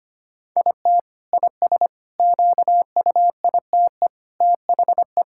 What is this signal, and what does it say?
Morse code, heard as a keyed tone, about 25 wpm, 700 Hz. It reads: IT IS QUITE THE
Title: Sentences from Top 1000 Words - Review Part 5 - 25wpm